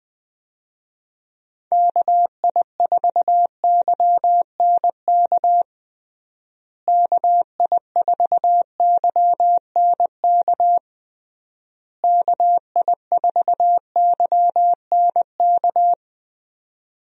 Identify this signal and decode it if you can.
Morse code, heard as a keyed tone, about 20 wpm, 700 Hz. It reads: KI4YNK KI4YNK KI4YNK
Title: USA Call Signs - 6 Characters - Repeated 3x - 20wpm